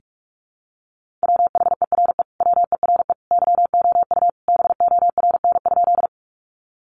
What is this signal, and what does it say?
Morse code, heard as a keyed tone, about 44 wpm, 700 Hz. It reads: WHERE WERE YOU BORN?
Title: Sentences from Top 1000 Words 45wpm